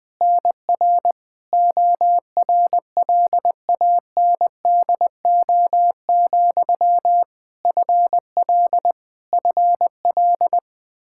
Morse code, heard as a keyed tone, about 21 wpm, 700 Hz. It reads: NR ORLANDO, FL FL
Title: Practice QSO - Exchange 1 - Beginning - 20wpm